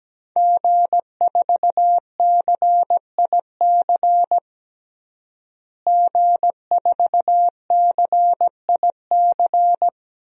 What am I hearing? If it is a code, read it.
Morse code, heard as a keyed tone, about 17 wpm, 700 Hz. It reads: G4CIC G4CIC
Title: DXCC-100 Call Signs - 5 Characters - Repeated 3x - 17wpm